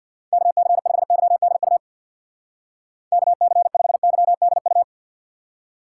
Morse code, heard as a keyed tone, about 62 wpm, 700 Hz. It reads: KC5YDU KC5YDU
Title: Speed Racing - USA 2x3 Call Signs 50wpm